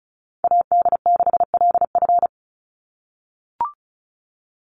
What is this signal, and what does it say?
Morse code, heard as a keyed tone, about 34 wpm, 700 Hz. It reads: AD6LF
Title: USA Call Signs 35wpm